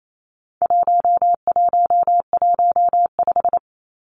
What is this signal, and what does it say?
Morse code, heard as a keyed tone, about 28 wpm, 700 Hz. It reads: 1115
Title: Numbers - 4-Digit 28wpm